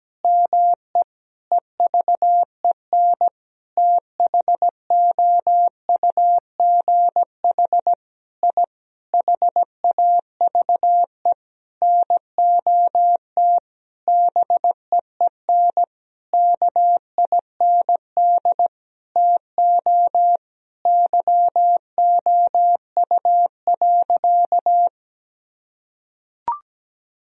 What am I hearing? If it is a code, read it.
Morse code, heard as a keyed tone, about 17 wpm, 700 Hz. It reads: ME EVEN THOUGH I HAVE NOT BEEN KIND TO YOU.
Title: Sentences from Top 1000 Words - Review Part 2 - 17wpm